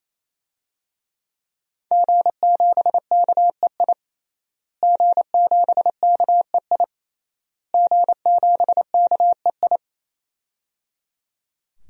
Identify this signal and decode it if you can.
Morse code, heard as a keyed tone, about 27 wpm, 700 Hz. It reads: G7KEI G7KEI G7KEI
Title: DXCC-100 Call Signs - 5 Characters - Repeated 3x - 28wpm